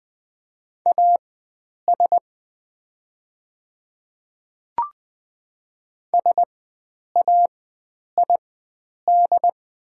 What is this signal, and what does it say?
Morse code, heard as a keyed tone, about 20 wpm, spaced as for 9 wpm, 700 Hz. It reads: AS SAID
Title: 4x Character Spacing - Top 300 Words - 20wpm